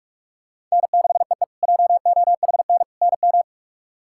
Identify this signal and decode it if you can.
Morse code, heard as a keyed tone, about 42 wpm, 700 Hz. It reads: N6EE JOHN NM
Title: K1USN SST QSO Practice - 45wpm